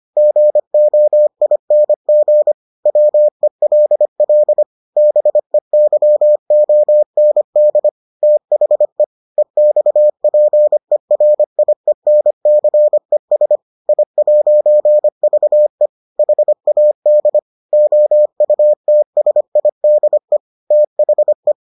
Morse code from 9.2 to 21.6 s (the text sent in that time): EXPERIENCES I'VE HAD OUTSIDE THE